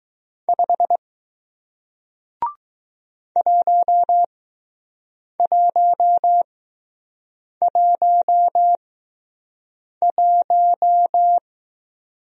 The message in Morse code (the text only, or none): 5 1111